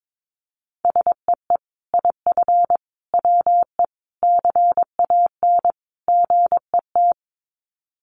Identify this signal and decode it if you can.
Morse code, heard as a keyed tone, about 22 wpm, 700 Hz. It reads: SEE IF WE CAN GET
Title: Sets of 5 Words Encore2 22wpm